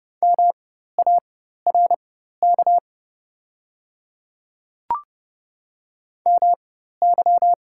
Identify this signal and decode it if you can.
Morse code, heard as a keyed tone, about 30 wpm, spaced as for 14 wpm, 690 Hz. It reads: MARK MY
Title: 4x Character Spacing - Top 500 Words - 30wpm